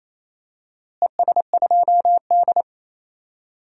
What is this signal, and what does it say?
Morse code, heard as a keyed tone, about 27 wpm, 700 Hz. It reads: ES2D
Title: DXCC-100 Call Signs - 4 Characters - Repeated 3x - 28wpm